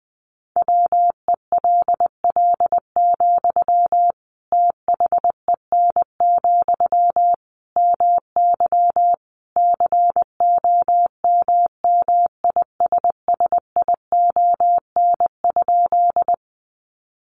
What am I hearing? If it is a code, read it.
Morse code, heard as a keyed tone, about 20 wpm, 700 Hz. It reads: WELL, THEN, MY COMMISSION?
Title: Sentences from Top 1400 Words 20wpm